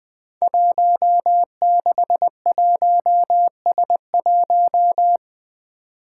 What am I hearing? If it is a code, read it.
Morse code, heard as a keyed tone, about 20 wpm, 700 Hz. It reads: 161S1
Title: Five Challenging Letter-Number Combinations 20wpm